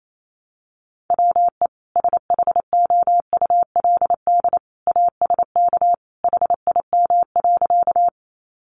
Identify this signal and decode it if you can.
Morse code, heard as a keyed tone, about 28 wpm, 700 Hz. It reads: WE SHOULD ASK HIM.
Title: Sentences from Top 300 Words 28wpm